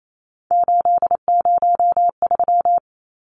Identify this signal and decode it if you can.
Morse code, heard as a keyed tone, about 28 wpm, 700 Hz. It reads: 803
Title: Numbers - 3-Digit 28wpm